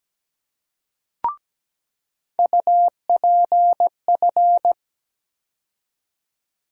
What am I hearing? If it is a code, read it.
Morse code, heard as a keyed tone, about 17 wpm, 700 Hz. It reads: UPF
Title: USA Call Sign Postfixes - 3 Letters 17wpm